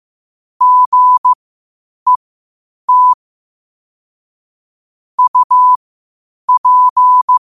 Morse code, heard as a keyed tone, about 15 wpm, spaced as for 8 wpm, 1000 Hz. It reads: GET UP
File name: qotdfiles / 8wpm